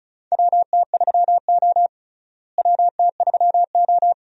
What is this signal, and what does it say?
Morse code, heard as a keyed tone, about 36 wpm, 700 Hz. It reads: WT3O WT3O
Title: USA Call Signs - 4 Characters - Repeated 2x - 35wpm